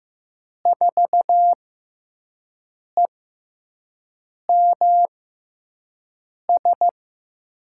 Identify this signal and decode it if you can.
Morse code, heard as a keyed tone, about 15 wpm, spaced as for 5 wpm, 700 Hz. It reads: 4EMS
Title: USA Call Signs - 6 Characters - 6x Character Spacing - 15wpm